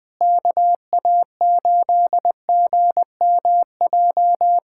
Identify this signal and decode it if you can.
Morse code, heard as a keyed tone, about 21 wpm, 700 Hz. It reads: KA8GMJ
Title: USA Callsign DE Callsign - 6 Characters - 20wpm